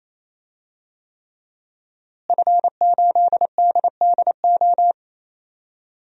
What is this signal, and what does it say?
Morse code, heard as a keyed tone, about 28 wpm, 700 Hz. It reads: F8DDO
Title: DXCC-100 Call Signs - 5 Characters - Repeated 3x - 28wpm